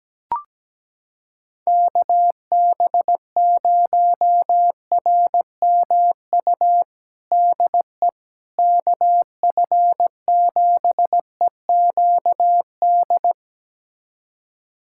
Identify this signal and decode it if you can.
Morse code, heard as a keyed tone, about 17 wpm, 700 Hz. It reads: KB0RMU DE KF7EQD
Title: USA Callsign DE Callsign - 6 Characters - 17wpm